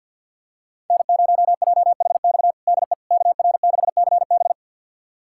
Morse code, heard as a keyed tone, about 50 wpm, 700 Hz. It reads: N0JHX DE KR6CB
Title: USA Callsign DE Callsign - 5 Characters - 50wpm